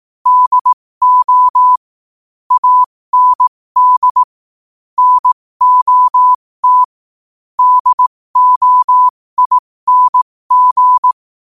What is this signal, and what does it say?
Morse code, heard as a keyed tone, about 18 wpm, spaced as for 15 wpm, 1000 Hz. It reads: DO AND NOT DOING
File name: qotdfiles / 15wpm